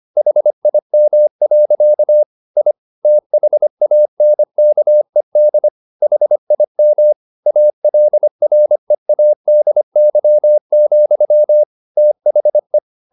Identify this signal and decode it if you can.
Morse code, heard as a keyed tone, about 25 wpm, 600 Hz. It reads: HIM. I THANKED HIM ALREADY, THE
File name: the old man and the sea part 1 25wpm